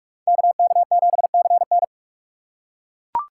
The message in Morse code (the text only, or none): KK7CN